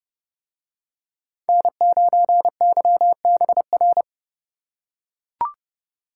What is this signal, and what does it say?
Morse code, heard as a keyed tone, about 30 wpm, 700 Hz. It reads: N9YBR